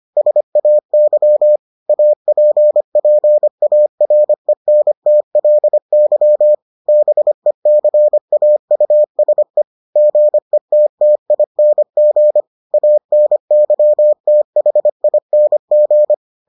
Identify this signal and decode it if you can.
Morse code, heard as a keyed tone, about 25 wpm, 600 Hz. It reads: SAY APPARENTLY BECAUSE GETTING ANYTHING